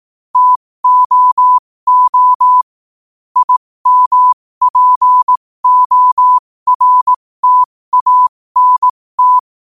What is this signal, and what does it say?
Morse code, heard as a keyed tone, about 18 wpm, spaced as for 15 wpm, 1000 Hz. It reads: TOO IMPORTANT
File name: qotdfiles / 15wpm